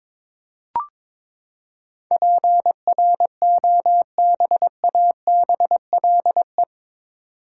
Morse code, heard as a keyed tone, about 22 wpm, 700 Hz. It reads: PROBABLE
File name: practice / 22wpm / Eight Letter Words - 22wpm